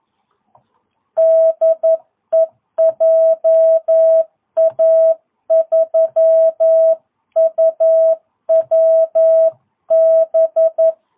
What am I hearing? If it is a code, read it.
Morse code, heard as a keyed tone, about 11 wpm, 650 Hz. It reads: DE JA3UWB
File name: jh8cbh / 2022cw